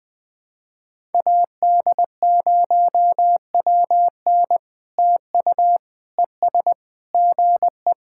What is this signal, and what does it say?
Morse code, heard as a keyed tone, about 20 wpm, 660 Hz. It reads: AD0WN TU ES GE
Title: POTA QSO Practice - 20wpm